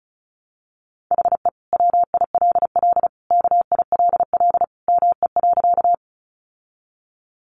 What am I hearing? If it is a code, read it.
Morse code, heard as a keyed tone, about 34 wpm, 700 Hz. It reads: HE WILL KILL ME.
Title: Sentences from Top 1400 Words 35wpm